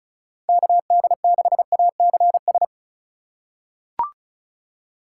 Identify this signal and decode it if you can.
Morse code, heard as a keyed tone, about 36 wpm, 700 Hz. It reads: KD6ACS
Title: USA Call Signs - 6 Characters - 4x Character Spacing - 35wpm